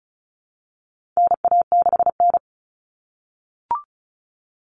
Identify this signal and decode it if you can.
Morse code, heard as a keyed tone, about 34 wpm, 700 Hz. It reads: NA6N
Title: Common Cut Numbers - 4-Digit 35wpm